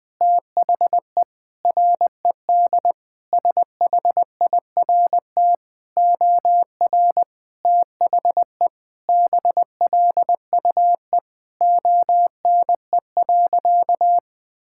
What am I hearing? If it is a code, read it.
Morse code, heard as a keyed tone, about 20 wpm, 700 Hz. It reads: THE RED SHIRT OR THE BLUE ONE.
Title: Sentences with 2-Word Phrases - Spoken Beginning - 20wpm